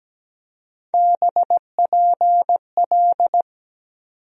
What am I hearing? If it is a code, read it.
Morse code, heard as a keyed tone, about 17 wpm, 700 Hz. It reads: BPL
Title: USA Call Sign Postfixes - 3 Letters 17wpm